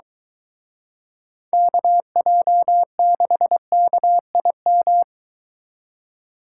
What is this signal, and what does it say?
Morse code, heard as a keyed tone, about 24 wpm, 700 Hz. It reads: KJ6KIM